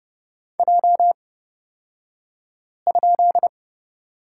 Morse code, heard as a keyed tone, about 30 wpm, spaced as for 5 wpm, 700 Hz. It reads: J?
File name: Lesson 37d - Letter x - Two Characters 30wpm